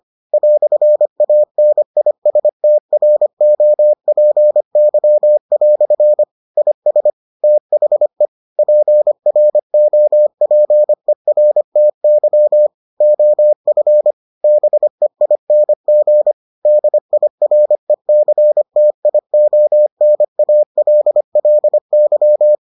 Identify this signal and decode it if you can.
Morse code, heard as a keyed tone, about 25 wpm, 600 Hz. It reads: "ANISTROPY" IS THE PROPERTY OF BEING DIRECTIONALLY